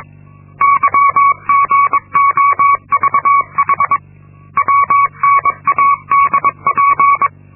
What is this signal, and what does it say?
Morse code, heard as a keyed tone, about 23 wpm, 1120 Hz. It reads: YGOVH WNADP
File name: Linksmakalnis-2